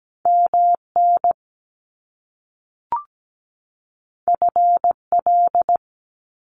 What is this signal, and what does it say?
Morse code, heard as a keyed tone, about 17 wpm, 700 Hz. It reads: MN FL
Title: US States 17wpm